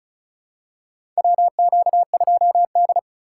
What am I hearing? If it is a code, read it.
Morse code, heard as a keyed tone, about 34 wpm, 700 Hz. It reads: WQ2D
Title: USA Call Signs - 2x Character Spacing - 35wpm